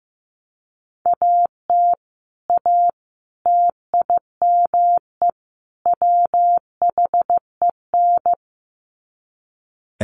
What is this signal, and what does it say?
Morse code, heard as a keyed tone, about 14 wpm, 700 Hz. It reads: AT A TIME WHEN
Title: Sets of 4 Words 15wpm